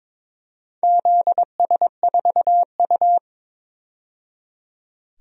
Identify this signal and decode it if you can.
Morse code, heard as a keyed tone, about 22 wpm, 700 Hz. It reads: ZS4U
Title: DXCC-100 Call Signs - 4 Characters - 22wpm